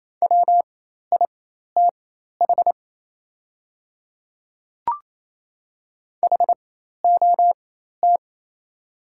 Morse code, heard as a keyed tone, about 28 wpm, spaced as for 13 wpm, 700 Hz. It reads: WITH HOT
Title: 4x Character Spacing - Top 100 Words - 28wpm